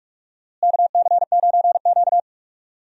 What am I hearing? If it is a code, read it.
Morse code, heard as a keyed tone, about 41 wpm, 700 Hz. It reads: KC9Q
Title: USA Call Signs - 4 Characters - Repeated 3x - 45wpm